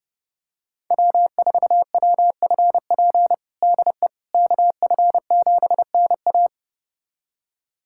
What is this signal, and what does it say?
Morse code, heard as a keyed tone, about 30 wpm, 700 Hz. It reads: W4WFP DE KF7NA
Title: USA Callsign DE Callsign - 5 Characters - 30wpm